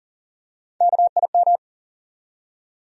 Morse code, heard as a keyed tone, about 37 wpm, 700 Hz. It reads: KIM
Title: USA Common Names - 40wpm